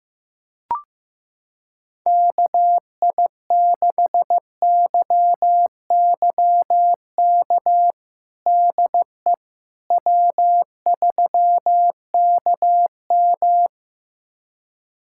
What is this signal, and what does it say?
Morse code, heard as a keyed tone, about 15 wpm, 700 Hz. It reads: KI6YYK DE W3KM
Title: USA Callsign DE Callsign 15wpm